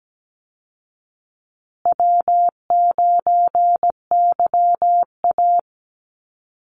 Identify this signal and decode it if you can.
Morse code, heard as a keyed tone, about 17 wpm, 700 Hz. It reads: W9YA
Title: Easy Call Signs Repeated 3x - 17wpm